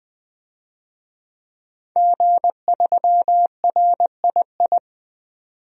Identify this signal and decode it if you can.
Morse code, heard as a keyed tone, about 20 wpm, 700 Hz. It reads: G3RII
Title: DXCC-100 Call Signs - 5 Characters - 20wpm